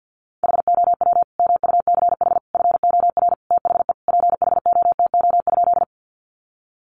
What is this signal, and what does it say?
Morse code, heard as a keyed tone, about 50 wpm, 700 Hz. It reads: HOW MUCH FOR THE PHOTO?
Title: Sentences from Top 1200 Words 50wpm